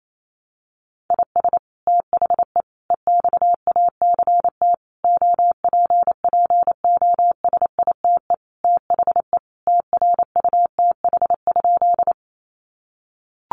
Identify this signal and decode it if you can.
Morse code, heard as a keyed tone, about 28 wpm, 700 Hz. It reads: IS THE EXACT OPPOSITE THE TRUTH?
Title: Sentences from Top 1300 Words 28wpm